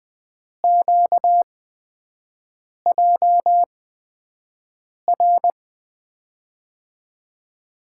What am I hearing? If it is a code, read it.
Morse code, heard as a keyed tone, about 20 wpm, spaced as for 5 wpm, 700 Hz. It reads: QJR
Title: USA Call Signs - 8x Character Spacing - 20wpm